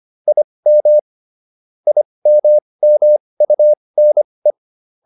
Morse code, heard as a keyed tone, about 25 wpm, spaced as for 20 wpm, 600 Hz. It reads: IM IMMUNE